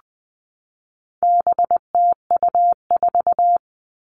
Morse code, heard as a keyed tone, about 20 wpm, 700 Hz. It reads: BTU4
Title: All Cut Numbers - 4-Digit 20wpm